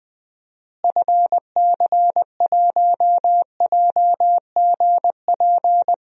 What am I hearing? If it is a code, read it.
Morse code, heard as a keyed tone, about 21 wpm, 700 Hz. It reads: FC1JGP
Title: DXCC-100 Call Signs - 6 Characters - Repeated 3x - 20wpm